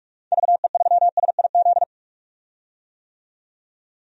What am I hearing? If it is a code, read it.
Morse code, heard as a keyed tone, about 44 wpm, 700 Hz. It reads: VE3SIZ
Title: DXCC-100 Call Signs - 6 Characters - 45wpm